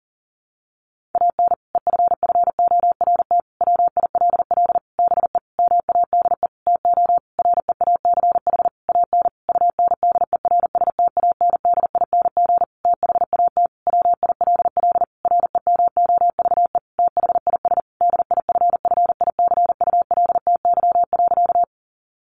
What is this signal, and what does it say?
Morse code, heard as a keyed tone, about 40 wpm, 700 Hz. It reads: AN EFFORT WILL BE MADE TO REACH AN UNDERSTANDING THAT WILL REMOVE THIS DIFFICULTY.
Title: Sentences from Top 1500 Words 40wpm